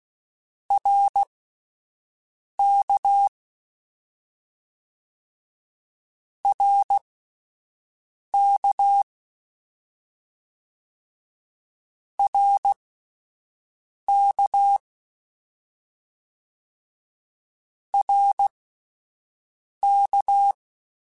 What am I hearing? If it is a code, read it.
Morse code, heard as a keyed tone, about 16 wpm, spaced as for 5 wpm, 800 Hz. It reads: RK RK RK RK
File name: Short Runs / Short Runs-mp3s / Balanced @10